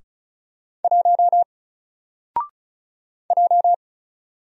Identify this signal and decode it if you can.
Morse code, heard as a keyed tone, about 36 wpm, 700 Hz. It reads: 1 J
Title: Single Character (Letter:Number:Punctuation:Prosign) Alt 35wpm